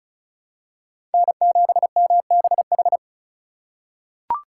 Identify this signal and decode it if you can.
Morse code, heard as a keyed tone, about 36 wpm, 700 Hz. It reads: N7MBH
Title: USA Call Signs - 5 Characters - 3x Character Spacing - 35wpm